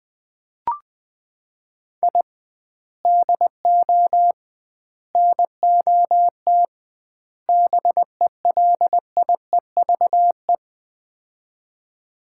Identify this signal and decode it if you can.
Morse code, heard as a keyed tone, about 20 wpm, 700 Hz. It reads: I DO NOT BELIEVE
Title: 2x Word-Spacing - Sets of 4 Words - 20wpm